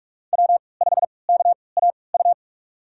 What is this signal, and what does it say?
Morse code, heard as a keyed tone, about 42 wpm, spaced as for 25 wpm, 700 Hz. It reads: W5KAU
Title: USA Call Signs - 5 Characters - 3x Character Spacing - 45wpm